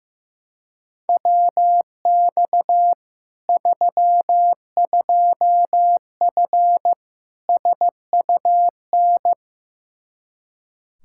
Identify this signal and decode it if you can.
Morse code, heard as a keyed tone, about 16 wpm, 700 Hz. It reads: WX 32F SUN
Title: Practice QSO - Exchange 2 - Info - 15wpm